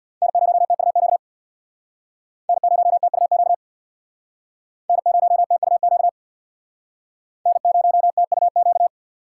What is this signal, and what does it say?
Morse code, heard as a keyed tone, about 64 wpm, 700 Hz. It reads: N0TUQ N0TUQ N0TUQ N0TUQ